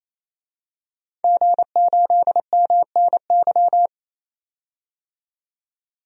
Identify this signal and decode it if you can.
Morse code, heard as a keyed tone, about 28 wpm, 690 Hz. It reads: G8MNY
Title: DXCC-100 Call Signs - 5 Characters - 28wpm